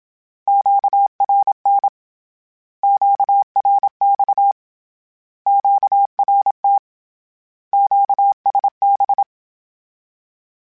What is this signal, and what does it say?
Morse code, heard as a keyed tone, about 27 wpm, 800 Hz. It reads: QRN QRX QRT QSB
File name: Misc / Misc-mp3s / Words @25